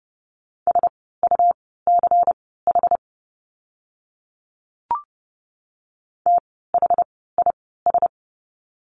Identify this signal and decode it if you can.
Morse code, heard as a keyed tone, about 30 wpm, spaced as for 17 wpm, 700 Hz. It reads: SUCH THIS
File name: course / 30:10wpm / 108 Lesson 18f - Letter m - Words 30wpm